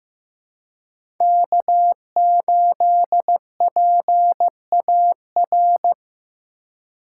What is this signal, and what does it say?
Morse code, heard as a keyed tone, about 15 wpm, 700 Hz. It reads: K8PAR
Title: USA Call Signs - 5 Characters - 3x Character Spacing - 15wpm